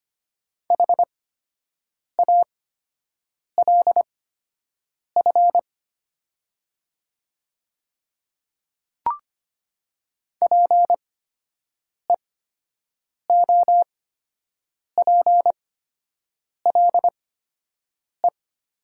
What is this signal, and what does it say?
Morse code, heard as a keyed tone, about 25 wpm, spaced as for 7 wpm, 700 Hz. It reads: HALF PEOPLE
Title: 8x Character Spacing - Top 500 Words - 25wpm